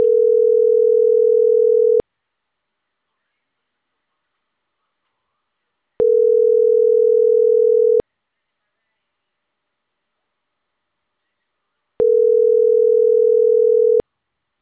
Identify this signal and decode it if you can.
Ringback tone (North America), 440 + 480 Hz.